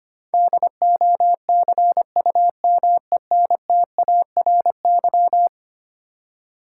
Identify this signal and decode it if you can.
Morse code, heard as a keyed tone, about 25 wpm, 700 Hz. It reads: DOCUMENTARY
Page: morse-code-ninja-practice-25wpm